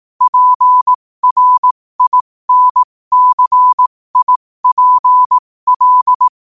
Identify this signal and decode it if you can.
Morse code, heard as a keyed tone, about 18 wpm, spaced as for 15 wpm, 1000 Hz. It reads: PRINCIPL
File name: qotdfiles / 15wpm